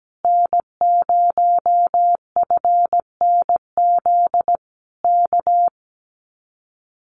Morse code, heard as a keyed tone, about 17 wpm, 700 Hz. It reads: N0FNZ K
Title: Common QSO Phrases Encore 17wpm